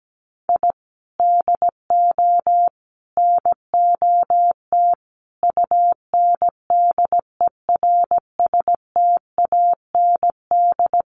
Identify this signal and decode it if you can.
Morse code, heard as a keyed tone, about 17 wpm, 700 Hz. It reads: I DO NOT UNDERSTAND
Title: Sets of 4 Words 17wpm